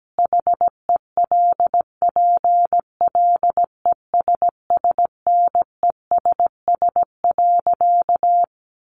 Morse code, heard as a keyed tone, about 17 wpm, 700 Hz. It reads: HELPLESSNESS.
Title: A Sound Of Thunder 17wpm